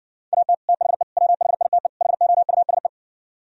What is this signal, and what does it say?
Morse code, heard as a keyed tone, about 69 wpm, 700 Hz. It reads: AT THE WHITE HOUSE